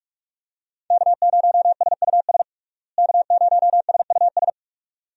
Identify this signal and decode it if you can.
Morse code, heard as a keyed tone, about 46 wpm, 700 Hz. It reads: K0SUS K0SUS
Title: USA Call Signs - 5 Characters - Repeated 2x - 45wpm